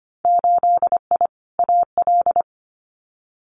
Morse code, heard as a keyed tone, about 25 wpm, 700 Hz. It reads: 8I AL
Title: Winter Field Day Exchanges Encore 25wpm